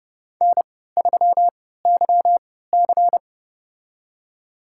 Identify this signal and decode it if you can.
Morse code, heard as a keyed tone, about 30 wpm, spaced as for 17 wpm, 700 Hz. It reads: N3YC
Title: USA Call Signs - 4 Characters - 3x Character Spacing - 30wpm